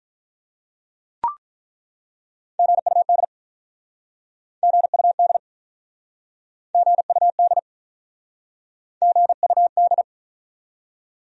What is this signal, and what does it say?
Morse code, heard as a keyed tone, about 48 wpm, 700 Hz. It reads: GUD GUD GUD GUD